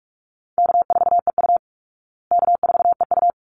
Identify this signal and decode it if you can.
Morse code, heard as a keyed tone, about 44 wpm, 700 Hz. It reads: K4EU K4EU
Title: Easy Call Signs Repeated 2x - 45wpm